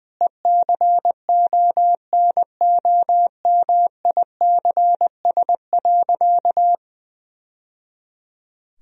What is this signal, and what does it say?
Morse code, heard as a keyed tone, about 20 wpm, 700 Hz. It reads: ECONOMICS.
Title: Sentences from Top 1000 Words - Review Part 3 - 20wpm